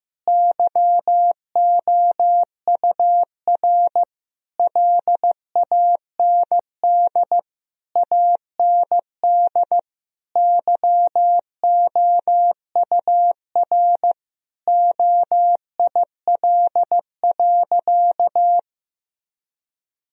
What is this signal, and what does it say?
Morse code, heard as a keyed tone, about 15 wpm, 700 Hz. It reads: YOUR LAND AND YOUR OIL.
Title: Sentences from Top 1000 Words - Review Part 4 - 15wpm